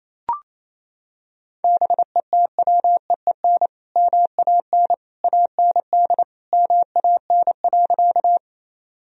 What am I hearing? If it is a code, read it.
Morse code, heard as a keyed tone, about 28 wpm, 700 Hz. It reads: BETWEEN MAN AND MAN.
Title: Sentences from Top 1000 Words - Review Part 1 - 28wpm